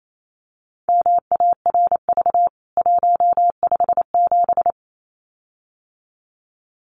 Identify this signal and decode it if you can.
Morse code, heard as a keyed tone, about 28 wpm, 700 Hz. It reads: MARV 157
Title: CWT Exchange 28wpm